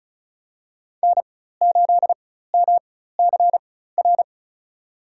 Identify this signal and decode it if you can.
Morse code, heard as a keyed tone, about 36 wpm, spaced as for 16 wpm, 700 Hz. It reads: N8MCR
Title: USA Call Signs - 4x Character Spacing - 35wpm